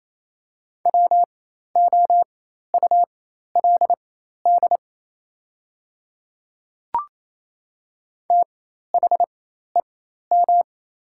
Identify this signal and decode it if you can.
Morse code, heard as a keyed tone, about 28 wpm, spaced as for 13 wpm, 700 Hz. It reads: WOULD THEM
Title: 4x Character Spacing - Top 100 Words - 28wpm